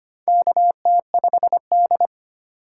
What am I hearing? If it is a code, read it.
Morse code, heard as a keyed tone, about 25 wpm, 700 Hz. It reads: KT5D
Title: USA Call Signs - 6x Character Spacing - 25wpm